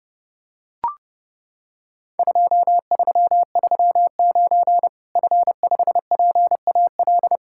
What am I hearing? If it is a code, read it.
Morse code, heard as a keyed tone, about 30 wpm, 700 Hz. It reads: 2339 F5PAL
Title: Worked All Europe DX Contest - QTC Exchange - 30wpm